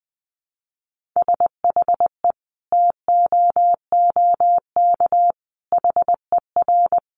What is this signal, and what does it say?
Morse code, heard as a keyed tone, about 20 wpm, 700 Hz. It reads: SHE TOOK HER
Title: Sentences from Top 900 Words 20wpm